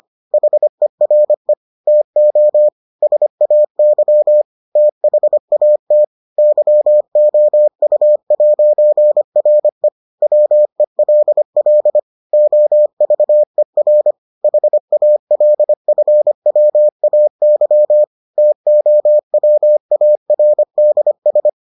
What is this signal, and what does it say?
Morse code, heard as a keyed tone, about 25 wpm, 600 Hz. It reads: HERE TO SAY THAT YOU'RE WELL OVER HALFWAY TOWARDS